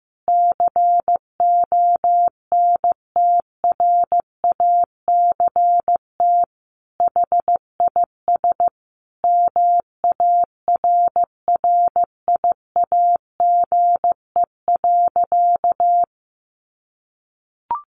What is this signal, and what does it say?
Morse code, heard as a keyed tone, about 15 wpm, 700 Hz. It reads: CONTRACT HIS MARRIAGE.
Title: Sentences from Top 1400 Words 15wpm